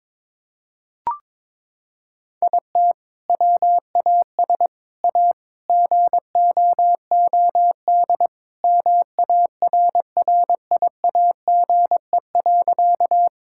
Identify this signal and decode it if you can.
Morse code, heard as a keyed tone, about 22 wpm, 700 Hz. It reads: IT WAS A GOOD MARRIAGE.
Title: Sentences from Top 1000 Words - Review Part 5 - 22wpm